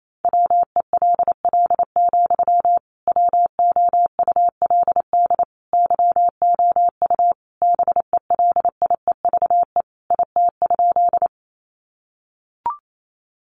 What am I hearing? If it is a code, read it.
Morse code, heard as a keyed tone, about 28 wpm, 700 Hz. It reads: WELL, WOULD YOU BELIEVE IT?
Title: Sentences from Top 1000 Words 28wpm